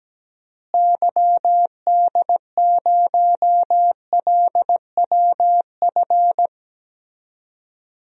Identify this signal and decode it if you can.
Morse code, heard as a keyed tone, about 17 wpm, 700 Hz. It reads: YD0LWF
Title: DXCC-100 Call Signs - 6 Characters - Repeated 3x - 17wpm